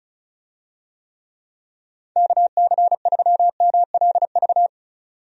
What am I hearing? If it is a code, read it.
Morse code, heard as a keyed tone, about 36 wpm, 700 Hz. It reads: KC3MLV
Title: USA Call Signs - 6 Characters - 35wpm